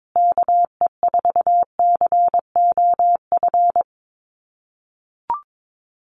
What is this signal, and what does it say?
Morse code, heard as a keyed tone, about 22 wpm, 700 Hz. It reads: KE4COF
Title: USA Call Signs Repeated 3x 22wpm